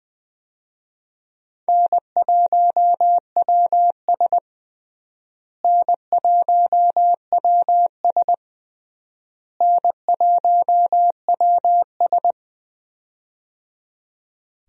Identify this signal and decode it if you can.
Morse code, heard as a keyed tone, about 21 wpm, 700 Hz. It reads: N1WS N1WS N1WS
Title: USA Call Signs - 4 Characters - Repeated 3x - 20wpm